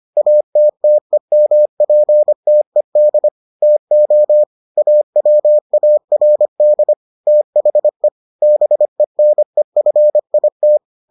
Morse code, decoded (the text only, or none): ATTEMPTED TO AWARD THE BENEFIT